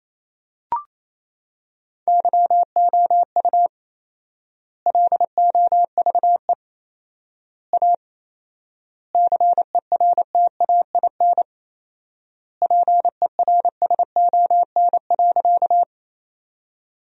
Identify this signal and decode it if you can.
Morse code, heard as a keyed tone, about 27 wpm, 700 Hz. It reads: YOU LOVE A CERTAIN PERSON.
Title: 4x Word-Spacing - Sentences from Top 500 Words - 28wpm